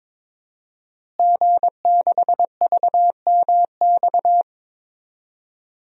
Morse code, heard as a keyed tone, about 22 wpm, 700 Hz. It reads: G6VMX
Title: DXCC-100 Call Signs - 5 Characters - Repeated 2x - 22wpm